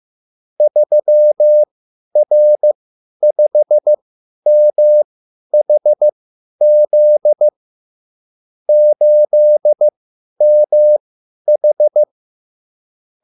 Morse code, heard as a keyed tone, about 15 wpm, spaced as for 10 wpm, 600 Hz. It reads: 3R5MHZ 8MH
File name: QSL / CW lesson